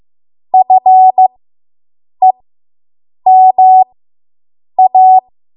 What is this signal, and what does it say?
Morse code, heard as a keyed tone, about 15 wpm, spaced as for 7 wpm, 750 Hz. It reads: FEMA